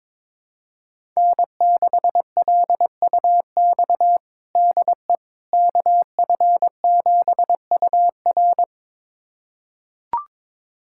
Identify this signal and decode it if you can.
Morse code, heard as a keyed tone, about 22 wpm, 700 Hz. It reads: N6LUX DE KF7UR
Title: USA Callsign DE Callsign - 5 Characters - 22wpm